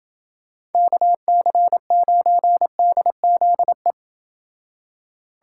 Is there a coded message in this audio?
Yes: Morse code from 0.7 to 3.9 s, heard as a keyed tone, about 29 wpm, 700 Hz. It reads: KC9DZE